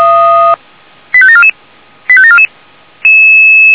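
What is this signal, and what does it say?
Five-tone selective call (ZVEI), heard as quick runs of five tones, tones of about 70 ms. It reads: 85744 85744
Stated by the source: ZVEI five-tone selective call received with an SDR.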